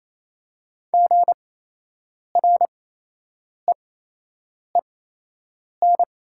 Morse code, heard as a keyed tone, about 28 wpm, spaced as for 8 wpm, 690 Hz. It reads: GREEN
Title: 8x Character Spacing - Top 500 Words - 28wpm